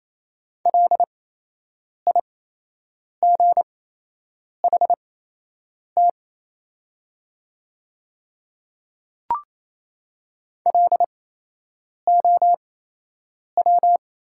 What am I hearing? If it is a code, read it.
Morse code, heard as a keyed tone, about 28 wpm, spaced as for 8 wpm, 690 Hz. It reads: LIGHT LOW
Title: 8x Character Spacing - Top 200 Words - 28wpm